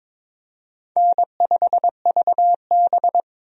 Morse code, heard as a keyed tone, about 22 wpm, 700 Hz. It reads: N5VB
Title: USA Call Signs - 4 Characters - 22wpm